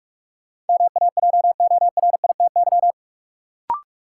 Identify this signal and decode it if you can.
Morse code, heard as a keyed tone, about 43 wpm, 700 Hz. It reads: MAJORITY
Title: Sets of 3 Words - New Word Drill - 45wpm